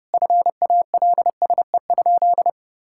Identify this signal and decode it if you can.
Morse code, heard as a keyed tone, about 32 wpm, 700 Hz. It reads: FALSE?
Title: Sentences from Top 1600 Words 30wpm